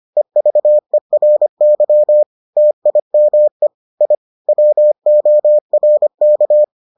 Morse code, heard as a keyed tone, about 25 wpm, 600 Hz. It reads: EVERY TIME I WORK